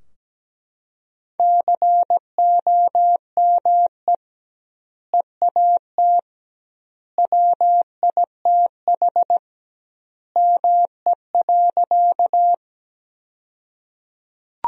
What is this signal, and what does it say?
Morse code, heard as a keyed tone, about 17 wpm, 700 Hz. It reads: COME EAT WITH ME.
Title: 2x Word-Spacing - Sentences from Top 300 Words - 17wpm